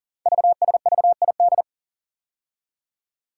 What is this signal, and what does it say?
Morse code, heard as a keyed tone, about 42 wpm, 690 Hz. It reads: VSVID
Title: Code Groups - 5 Letters - 40wpm